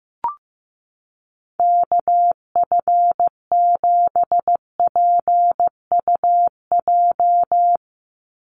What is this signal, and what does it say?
Morse code, heard as a keyed tone, about 15 wpm, 700 Hz. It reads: KF7PUJ
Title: USA Call Signs 15wpm